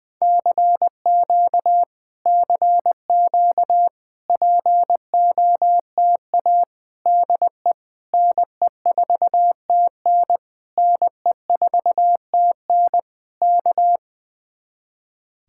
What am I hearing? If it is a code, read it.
Morse code, heard as a keyed tone, about 20 wpm, 700 Hz. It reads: CQ CQ POTA DE NE4TN NE4TN K
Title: Speed-Racing Common QSO Phrases - 20wpm